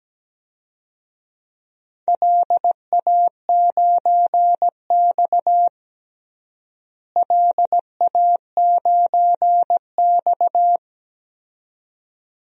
Morse code, heard as a keyed tone, about 17 wpm, 700 Hz. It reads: LA9X LA9X
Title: DXCC-100 Call Signs - 4 Characters - Repeated 2x - 17wpm